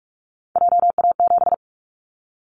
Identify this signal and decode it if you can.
Morse code, heard as a keyed tone, about 46 wpm, 700 Hz. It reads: JA7
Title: Callsign Prefixes for Top 50 DXCC - 45wpm